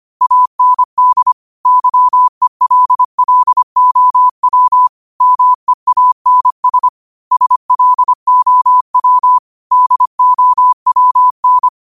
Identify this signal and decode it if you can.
Morse code, heard as a keyed tone, about 25 wpm, 1000 Hz. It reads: AND YELLOW MEANS SLOW DOWN